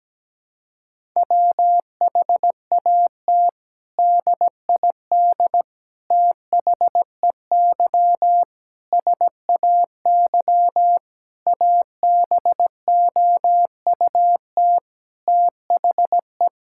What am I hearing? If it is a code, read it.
Morse code, heard as a keyed tone, about 17 wpm, 700 Hz. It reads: WHAT DID THEY SAY ABOUT THE
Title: Sentences from Top 1000 Words - Review Part 5 - 17wpm